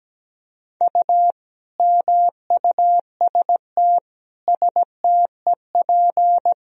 Morse code, heard as a keyed tone, about 17 wpm, 700 Hz. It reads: U MUST STEP
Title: Sentences from Top 400 Words and Abbreviations 17wpm